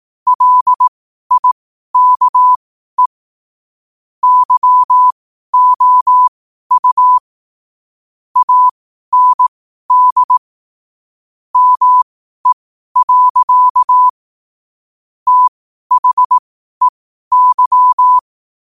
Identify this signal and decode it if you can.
Morse code, heard as a keyed tone, about 18 wpm, spaced as for 12 wpm, 1000 Hz. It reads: LIKE YOU AND ME. THEY
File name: qotdfiles / 12wpm